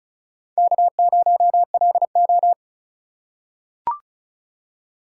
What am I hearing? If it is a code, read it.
Morse code, heard as a keyed tone, about 34 wpm, 700 Hz. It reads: K0LO
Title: USA Call Signs - 4 Characters - 8x Character Spacing - 35wpm